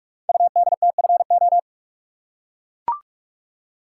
Morse code, heard as a keyed tone, about 44 wpm, 690 Hz. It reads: UDTFO
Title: Code Groups - 5 Letters - 45wpm